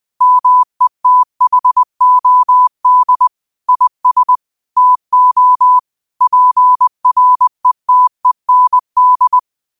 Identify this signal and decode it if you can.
Morse code, heard as a keyed tone, about 20 wpm, 1000 Hz. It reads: METHOD IS TO PRETEND
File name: qotdfiles / 20wpm